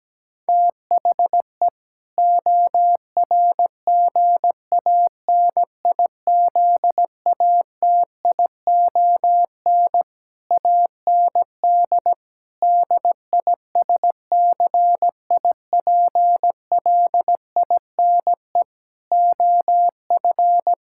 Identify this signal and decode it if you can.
Morse code, heard as a keyed tone, about 17 wpm, 700 Hz. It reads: THE ORGANIZATION AND DISCIPLINE OF
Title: Sentences from Top 1000 Words - Review Part 6 - 17wpm